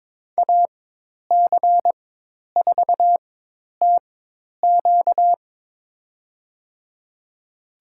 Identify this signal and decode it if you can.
Morse code, heard as a keyed tone, about 22 wpm, spaced as for 10 wpm, 700 Hz. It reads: AC4TQ
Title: USA Call Signs - 4x Character Spacing - 22wpm